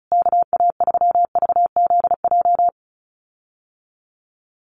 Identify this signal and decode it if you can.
Morse code, heard as a keyed tone, about 34 wpm, 700 Hz. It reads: KA3VZJ
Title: USA Call Signs Repeated 3x 35wpm